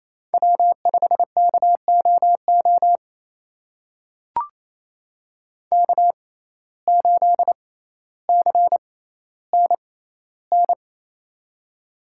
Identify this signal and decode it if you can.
Morse code, heard as a keyed tone, about 28 wpm, spaced as for 14 wpm, 700 Hz. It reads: W5KOO K8CNN
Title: USA Call Signs - 6x Character Spacing - 28wpm